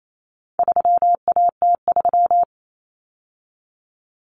Morse code, heard as a keyed tone, about 28 wpm, 700 Hz. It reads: 3AT3
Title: Common Cut Numbers - 4-Digit 28wpm